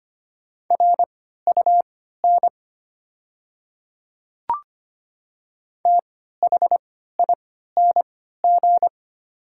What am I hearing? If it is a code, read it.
Morse code, heard as a keyed tone, about 25 wpm, spaced as for 14 wpm, 690 Hz. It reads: RUN THING
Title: 3x Character Spacing - Top 300 Words - 25wpm